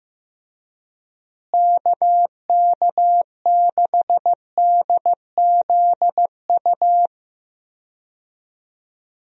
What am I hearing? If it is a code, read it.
Morse code, heard as a keyed tone, about 15 wpm, 700 Hz. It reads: KK6DZU
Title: USA Call Signs - 6 Characters - Repeated 3x - 15wpm